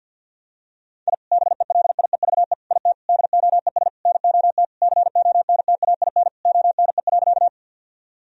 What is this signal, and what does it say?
Morse code, heard as a keyed tone, about 50 wpm, 700 Hz. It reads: I BELIEVE IT DOES NOT CONTAIN ONE.
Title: Sentences from Top 1000 Words - Review Part 2 - 50wpm